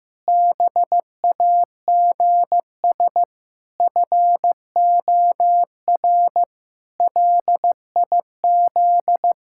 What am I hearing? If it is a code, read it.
Morse code, heard as a keyed tone, about 16 wpm, 700 Hz. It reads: BAGS FOR LIZ
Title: Pangram Challenge v2 - 15wpm